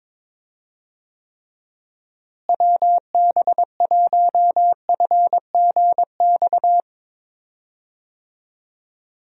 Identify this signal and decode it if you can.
Morse code, heard as a keyed tone, about 22 wpm, 700 Hz. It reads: WB1FGX
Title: USA Call Signs - 6 Characters - Repeated 3x - 22wpm